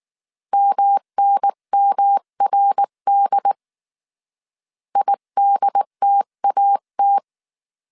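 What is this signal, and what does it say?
Morse code, heard as a keyed tone, about 19 wpm, 800 Hz. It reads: MNMRD IDTAT
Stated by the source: Morse code dictation at 18 wpm from the IW3SGT QRS Morse code training course, five-character groups of 300 characters.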